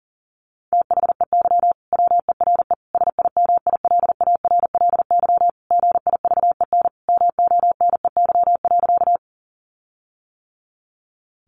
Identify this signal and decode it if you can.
Morse code, heard as a keyed tone, about 40 wpm, 700 Hz. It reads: THEY WERE SIMILARLY GIVEN MONEY.
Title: Sentences from Top 1400 Words 40wpm